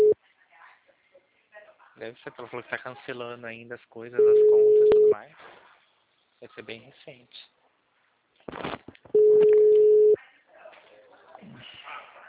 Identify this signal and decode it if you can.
Ringback tone (Europe, 425 Hz), with speech.
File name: out- -2013- - - .72338